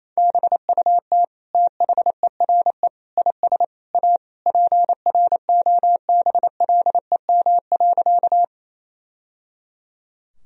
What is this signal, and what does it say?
Morse code, heard as a keyed tone, about 28 wpm, 700 Hz. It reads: BUT THERE IS A PROBLEM.
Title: Sentences from Top 1000 Words - Review Part 2 - 28wpm